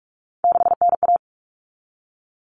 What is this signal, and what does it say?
Morse code, heard as a keyed tone, about 44 wpm, 700 Hz. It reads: 6NA
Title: Common Cut Numbers - 3-Digit 45wpm